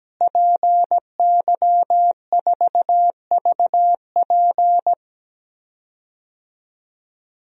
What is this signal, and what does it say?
Morse code, heard as a keyed tone, about 17 wpm, 700 Hz. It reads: PY4VP
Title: DXCC-100 Call Signs - 5 Characters - Repeated 3x - 17wpm